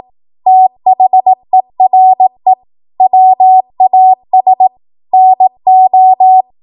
Morse code, heard as a keyed tone, about 18 wpm, 750 Hz. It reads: THERE WAS NO
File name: Morse / 18WPM